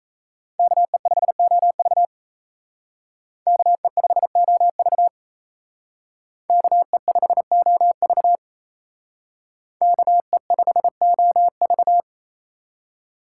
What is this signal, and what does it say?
Morse code, heard as a keyed tone, about 32 wpm, 700 Hz. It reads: KE5OV KE5OV KE5OV KE5OV
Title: Speed Racing - USA 2x2 Call Signs 28wpm